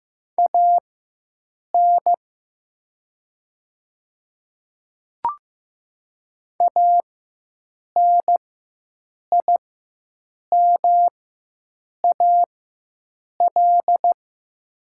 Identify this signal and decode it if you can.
Morse code, heard as a keyed tone, about 15 wpm, spaced as for 7 wpm, 700 Hz. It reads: AN ANIMAL
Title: 4x Character Spacing - Top 200 Words - 15wpm